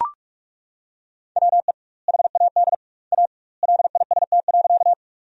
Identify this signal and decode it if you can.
Morse code, heard as a keyed tone, about 41 wpm, 700 Hz. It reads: WE HAD A LIST.
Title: 2x Word-Spacing - Sentences from Top 400 Words - 45wpm